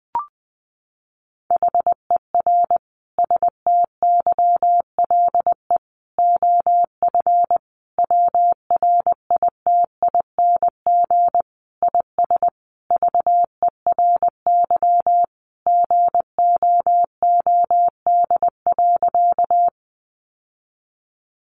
Morse code, heard as a keyed tone, about 20 wpm, 700 Hz. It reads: HER STYLE OF WRITING IS VERY GOOD.
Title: Sentences from Top 800 Words 20wpm